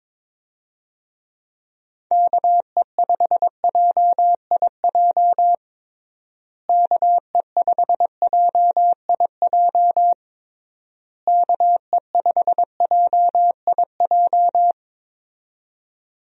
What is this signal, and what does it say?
Morse code, heard as a keyed tone, about 22 wpm, 700 Hz. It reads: KE5JIJ KE5JIJ KE5JIJ
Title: USA Call Signs - 6 Characters - Repeated 3x - 22wpm